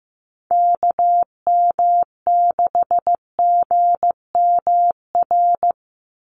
Morse code, heard as a keyed tone, about 15 wpm, 700 Hz. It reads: KM6GMR
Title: USA Call Signs 15wpm